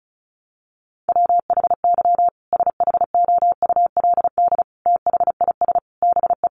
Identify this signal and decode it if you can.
Morse code, heard as a keyed tone, about 34 wpm, 700 Hz. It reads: WHY SHOULD THIS BE
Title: Sentences from Top 300 Words 35wpm